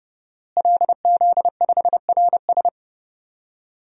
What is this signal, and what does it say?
Morse code, heard as a keyed tone, about 30 wpm, 700 Hz. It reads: LZ5RS
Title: DXCC-100 Call Signs - 5 Characters - Repeated 2x - 30wpm